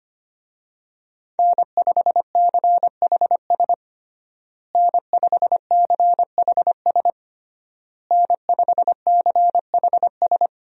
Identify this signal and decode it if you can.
Morse code, heard as a keyed tone, about 25 wpm, 700 Hz. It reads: N5CHS N5CHS N5CHS
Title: USA Call Signs - 5 Characters - Repeated 3x - 25wpm